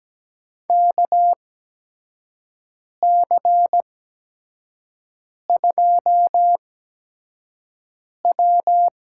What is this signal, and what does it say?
Morse code, heard as a keyed tone, about 17 wpm, spaced as for 5 wpm, 700 Hz. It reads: KC2W
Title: USA Call Signs - 6 Characters - 8x Character Spacing - 17wpm